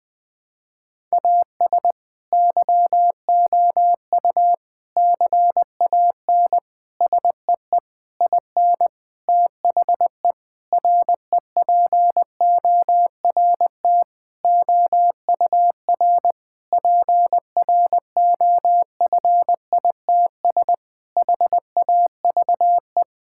Morse code, heard as a keyed tone, about 19 wpm, 700 Hz. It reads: AS YOU CAN SEE IN THE REPORT OUR PROFITS HAVE
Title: Sentences with 6-Word Phrases - 20wpm